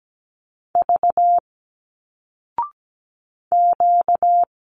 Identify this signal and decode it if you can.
Morse code, heard as a keyed tone, about 17 wpm, 700 Hz. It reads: V Q
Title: Single Letters 17wpm